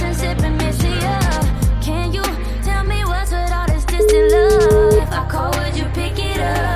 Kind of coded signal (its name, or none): telephone call progress tone